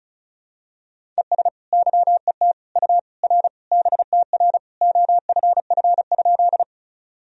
Morse code, heard as a keyed tone, about 36 wpm, 700 Hz. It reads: ES YET U R BTR OFF?
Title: Sentences from Top 500 Words and Abbreviations 35wpm